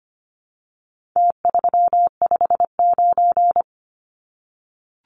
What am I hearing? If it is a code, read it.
Morse code, heard as a keyed tone, about 25 wpm, 700 Hz. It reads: T359
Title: Numbers - Contest Style - 4-Digit 25wpm